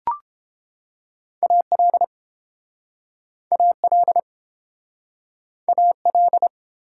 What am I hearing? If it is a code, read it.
Morse code, heard as a keyed tone, about 30 wpm, 700 Hz. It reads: AL AL AL